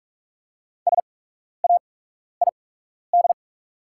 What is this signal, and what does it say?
Morse code, heard as a keyed tone, about 44 wpm, spaced as for 12 wpm, 700 Hz. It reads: SAID
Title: 8x Character Spacing - Top 100 Words - 45wpm